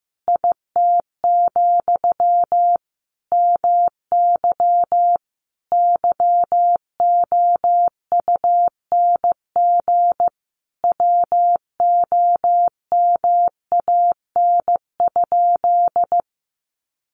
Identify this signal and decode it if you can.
Morse code, heard as a keyed tone, about 15 wpm, 700 Hz. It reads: IT, MY YOUNG WOMAN?
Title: Sentences from Top 1100 Words 15wpm